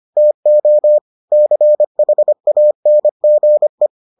Morse code, heard as a keyed tone, about 25 wpm, 600 Hz. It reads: TO CHANGE